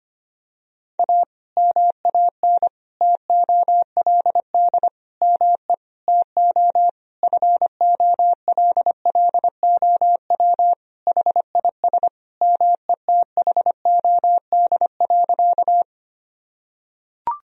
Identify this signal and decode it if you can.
Morse code, heard as a keyed tone, about 25 wpm, 700 Hz. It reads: A MAN TOLD ME TO FOLLOW HIS METHOD.
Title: Sentences from Top 1000 Words - Review Part 3 - 25wpm